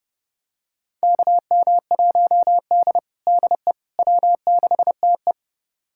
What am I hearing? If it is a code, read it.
Morse code, heard as a keyed tone, about 32 wpm, 700 Hz. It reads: KM1D DE W6TE
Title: USA Callsign DE Callsign - 4 Characters - 30wpm